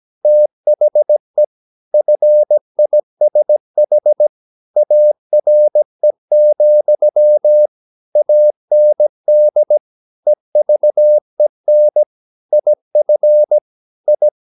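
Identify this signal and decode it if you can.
Morse code, heard as a keyed tone, about 17 wpm, 600 Hz. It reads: THE FISH ARE, AND EVEN IF I
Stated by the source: 17 wpm Morse practice sending the English text of a novel.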